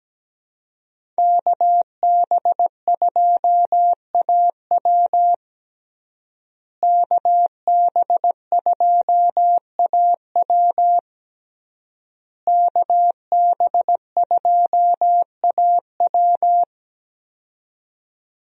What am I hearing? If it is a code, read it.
Morse code, heard as a keyed tone, about 17 wpm, 700 Hz. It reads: KB2AW KB2AW KB2AW
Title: USA Call Signs - 5 Characters - Repeated 3x - 17wpm